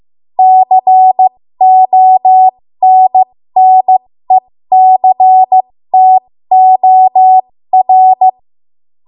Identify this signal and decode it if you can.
Morse code, heard as a keyed tone, about 15 wpm, spaced as for 13 wpm, 750 Hz. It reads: CONNECTOR
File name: Morse / 13WPM